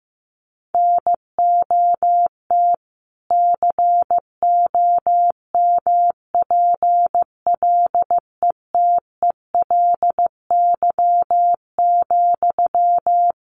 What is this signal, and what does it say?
Morse code, heard as a keyed tone, about 15 wpm, 700 Hz. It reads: NOT COMPLETELY,
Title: Sentences from Top 1100 Words 15wpm